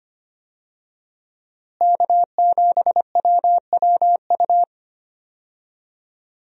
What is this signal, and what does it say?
Morse code, heard as a keyed tone, about 25 wpm, 700 Hz. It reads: K7WWU
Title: USA Call Signs - 5 Characters - Repeated 2x - 25wpm